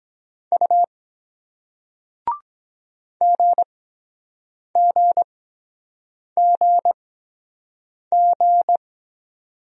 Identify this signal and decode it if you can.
Morse code, heard as a keyed tone, about 22 wpm, spaced as for 7 wpm, 700 Hz. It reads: U GGGG